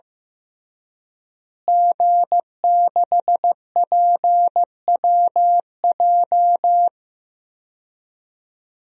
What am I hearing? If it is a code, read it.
Morse code, heard as a keyed tone, about 15 wpm, 700 Hz. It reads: G6PWJ
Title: DXCC-100 Call Signs - 5 Characters - Repeated 3x - 15wpm